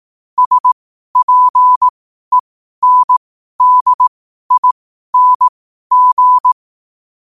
Morse code, heard as a keyed tone, about 18 wpm, spaced as for 12 wpm, 1000 Hz. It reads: SPENDING
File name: qotdfiles / 12wpm